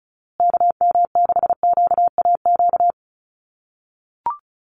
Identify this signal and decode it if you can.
Morse code, heard as a keyed tone, about 34 wpm, 700 Hz. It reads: KM6QAQ
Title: USA Call Signs Repeated 3x 35wpm